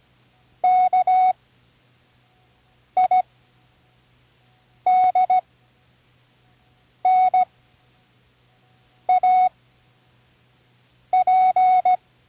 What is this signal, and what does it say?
Morse code, heard as a keyed tone, about 16 wpm, spaced as for 5 wpm, 740 Hz. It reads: KIDNAP
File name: Learning Morse Code (from Tune in the World)